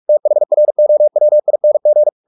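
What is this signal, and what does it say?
Morse code, heard as a keyed tone, about 44 wpm, 600 Hz. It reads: THROWING
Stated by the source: Morse practice at 45 wpm, the novel Never by Ken Follett in English, from CW med Gnister.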